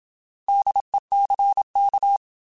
Morse code, heard as a keyed tone, about 27 wpm, 800 Hz. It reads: DECK
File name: Alphabetical / Alphabetical mp3s / Ds @25